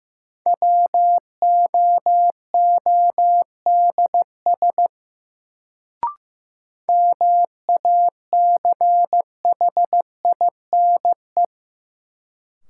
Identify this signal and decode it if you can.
Morse code, heard as a keyed tone, about 15 wpm, 700 Hz. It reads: WOODS MACHINE
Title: Sentences from Top 1000 Words - Review Part 2 - New Word Drill - 15wpm